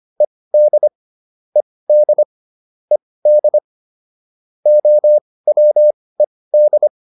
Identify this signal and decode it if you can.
Morse code, heard as a keyed tone, about 25 wpm, spaced as for 18 wpm, 600 Hz. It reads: ED ED ED OWED